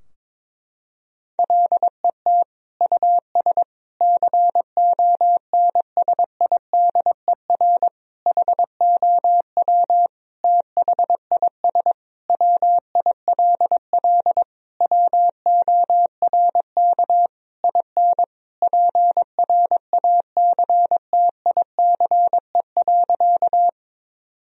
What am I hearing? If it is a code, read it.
Morse code, heard as a keyed tone, about 22 wpm, 700 Hz. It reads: LET US CONSIDER HOW THIS WILL WORK IN PRACTICE.
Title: Sentences from Top 1000 Words - Review Part 4 - 22wpm